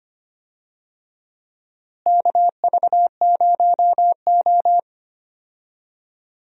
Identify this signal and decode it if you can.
Morse code, heard as a keyed tone, about 25 wpm, 700 Hz. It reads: KV0O
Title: USA Call Signs - 4 Characters - Repeated 2x - 25wpm